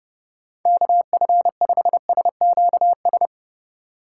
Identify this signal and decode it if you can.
Morse code, heard as a keyed tone, about 33 wpm, 700 Hz. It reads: KF5SQS